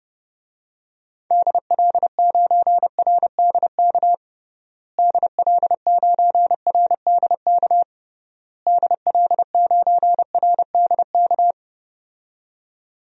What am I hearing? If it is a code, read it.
Morse code, heard as a keyed tone, about 32 wpm, 700 Hz. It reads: DL9RDK DL9RDK DL9RDK
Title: DXCC-100 Call Signs - 6 Characters - Repeated 3x - 30wpm